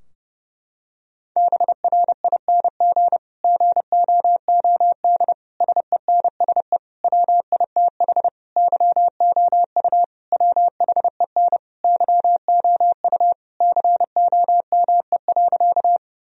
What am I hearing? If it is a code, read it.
Morse code, heard as a keyed tone, about 30 wpm, 700 Hz. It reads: BRING GOOD SENSE WITH YOU WHEN YOU COME.
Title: Sentences from Top 1000 Words - Review Part 4 - 30wpm